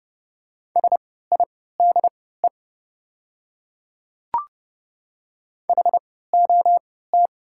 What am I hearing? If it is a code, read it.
Morse code, heard as a keyed tone, about 30 wpm, spaced as for 17 wpm, 700 Hz. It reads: SIDE HOT
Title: 3x Character Spacing - Top 100 Words - 30wpm